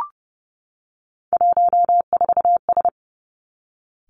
Morse code, heard as a keyed tone, about 30 wpm, 700 Hz. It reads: 14S
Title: Three Challenging Letter-Number Combinations 30wpm